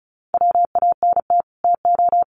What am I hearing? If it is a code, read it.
Morse code, heard as a keyed tone, about 34 wpm, 700 Hz. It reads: WANT TO
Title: Sets of 2 Words 35wpm